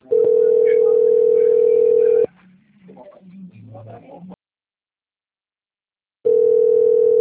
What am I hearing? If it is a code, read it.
Ringback tone (North America), 440 + 480 Hz.